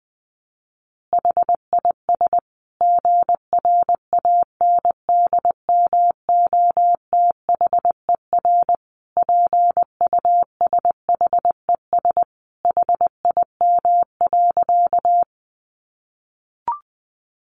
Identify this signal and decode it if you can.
Morse code, heard as a keyed tone, about 19 wpm, 700 Hz. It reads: HIS GRANDMOTHER PUSHES HIM.
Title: Additional Easy Phrases 20wpm